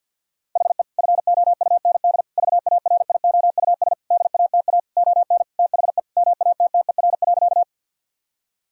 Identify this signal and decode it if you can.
Morse code, heard as a keyed tone, about 50 wpm, 700 Hz. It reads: HE FOUND VARIOUS DATA ON THE MATTER.
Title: Sentences from Top 1000 Words - Review Part 4 - 50wpm